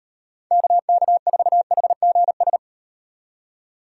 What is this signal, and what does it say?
Morse code, heard as a keyed tone, about 38 wpm, 700 Hz. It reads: KK4HGS